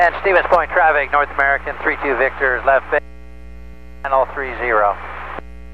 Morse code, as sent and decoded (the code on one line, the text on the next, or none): none